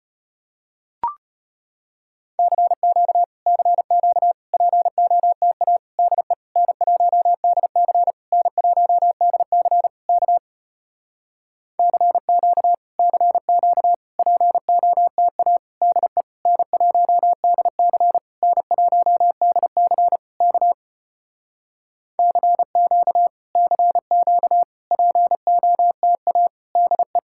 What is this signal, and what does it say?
Morse code, heard as a keyed tone, about 33 wpm, 700 Hz. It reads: CQ CQ POTA DE N1DC N1DC K CQ CQ POTA DE N1DC N1DC K CQ CQ POTA DE